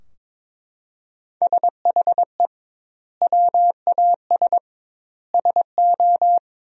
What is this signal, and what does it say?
Morse code, heard as a keyed tone, about 22 wpm, 700 Hz. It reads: SHE WAS SO
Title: 2x Word-Spacing - Sentences from Top 400 Words - 22wpm